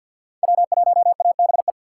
Morse code, heard as a keyed tone, about 53 wpm, 690 Hz. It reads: W1ABE
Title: USA Call Signs - 2x Character Spacing - 50wpm